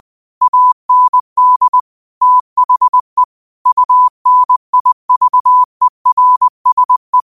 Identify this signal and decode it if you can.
Morse code, heard as a keyed tone, about 20 wpm, 1000 Hz. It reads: AND THE UNIVERSE
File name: qotdfiles / 20wpm